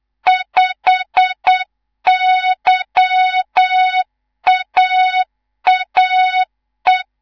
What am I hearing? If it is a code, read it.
Morse code, heard as a keyed tone, about 8 wpm, 3700 Hz, with noise. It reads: 5YAAE
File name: Morze-HA2MN / hang / 19 GYAKORLAT